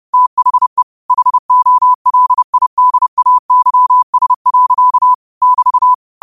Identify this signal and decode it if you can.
Morse code, heard as a keyed tone, about 30 wpm, 1000 Hz. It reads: THE HOLIDAYS. <BT>
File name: qotdfiles / 30wpm